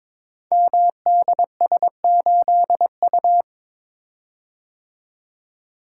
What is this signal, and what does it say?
Morse code, heard as a keyed tone, about 22 wpm, 700 Hz. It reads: MDS8U
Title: Code Groups - 5 Letter:Numbers - 22wpm